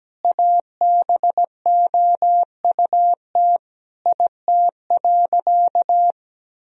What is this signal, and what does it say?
Morse code, heard as a keyed tone, about 17 wpm, 700 Hz. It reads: ABOUT IT.
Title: Sentences from Top 1000 Words - Review Part 2 - 17wpm